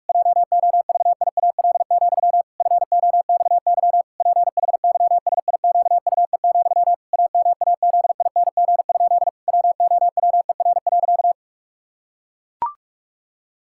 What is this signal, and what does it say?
Morse code, heard as a keyed tone, about 41 wpm, 700 Hz. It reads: JOVIAL, FOXY PHYSIQUE, AMAZING? WOWER.
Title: Pangram Challenge v2 - 45wpm